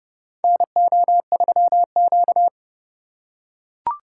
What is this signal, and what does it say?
Morse code, heard as a keyed tone, about 30 wpm, 700 Hz. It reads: NO3Q
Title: USA Call Signs - 4 Characters - 3x Character Spacing - 30wpm